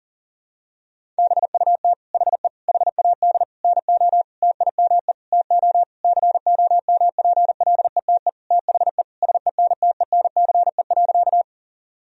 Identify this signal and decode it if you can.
Morse code, heard as a keyed tone, about 40 wpm, 700 Hz. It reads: BUT HE HAD NO TIME TO COMPLETE THE SENTENCE.
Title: Sentences from Top 1000 Words - Review Part 1 - 40wpm